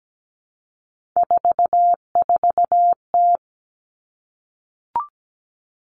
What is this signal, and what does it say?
Morse code, heard as a keyed tone, about 17 wpm, 700 Hz. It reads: 44T
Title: Three Challenging Letter-Number Combinations 17wpm